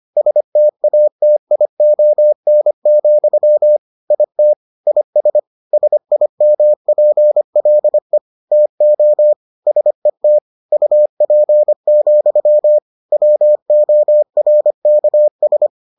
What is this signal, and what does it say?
Morse code, heard as a keyed tone, about 25 wpm, 600 Hz. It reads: STATION, IT IS SIMPLE TO SET UP, WORKS